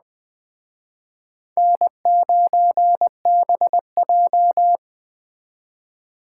Morse code, heard as a keyed tone, about 20 wpm, 700 Hz. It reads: N9BJ
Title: Speed Racing - USA 1x2 Call Signs 17wpm